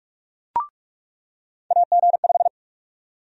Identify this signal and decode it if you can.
Morse code, heard as a keyed tone, about 46 wpm, 700 Hz. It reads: AG5
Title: USA Call Sign Prefixes - 45wpm